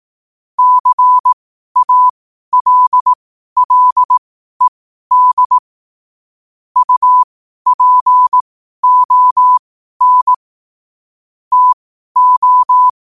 Morse code, heard as a keyed tone, about 18 wpm, spaced as for 12 wpm, 1000 Hz. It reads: CALLED UPON TO
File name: qotdfiles / 12wpm